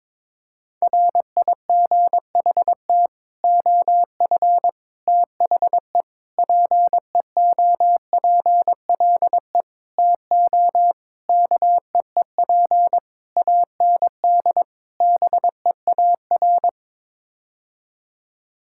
Morse code, heard as a keyed tone, about 22 wpm, 700 Hz. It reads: RIGHT OF THE PEOPLE TO KEEP AND BEAR
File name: practice / 22wpm / Sets of 8 Words - 22wpm